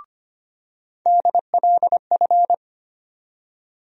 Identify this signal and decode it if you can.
Morse code, heard as a keyed tone, about 25 wpm, 700 Hz. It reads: DLF
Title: USA Call Sign Postfixes - 3 Letters 25wpm